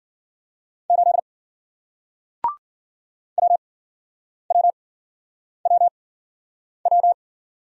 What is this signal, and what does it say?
Morse code, heard as a keyed tone, about 54 wpm, spaced as for 9 wpm, 700 Hz. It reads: 8 WWWW